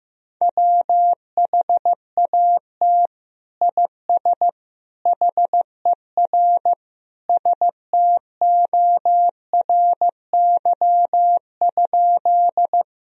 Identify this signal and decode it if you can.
Morse code, heard as a keyed tone, about 15 wpm, 700 Hz. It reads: WHAT IS HER STORY?
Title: Sentences from Top 1000 Words - Review Part 1 - 15wpm